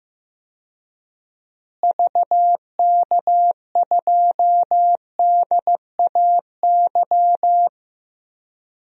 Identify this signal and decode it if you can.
Morse code, heard as a keyed tone, about 15 wpm, 700 Hz. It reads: VK2DAY
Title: DXCC-100 Call Signs - 6 Characters - Repeated 3x - 15wpm